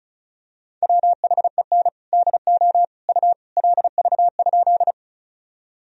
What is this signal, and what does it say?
Morse code, heard as a keyed tone, about 34 wpm, 700 Hz. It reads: WHEN DO U LV?
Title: Sentences from Top 400 Words and Abbreviations 35wpm